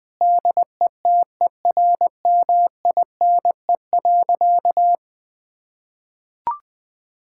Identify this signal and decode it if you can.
Morse code, heard as a keyed tone, about 21 wpm, 700 Hz. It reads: DETERMINE.
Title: Sentences from Top 1000 Words - Review Part 5 - 20wpm